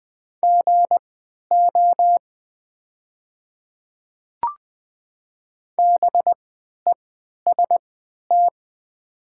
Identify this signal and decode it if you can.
Morse code, heard as a keyed tone, about 20 wpm, spaced as for 11 wpm, 700 Hz. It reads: GO BEST
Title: 3x Character Spacing - Top 500 Words - 20wpm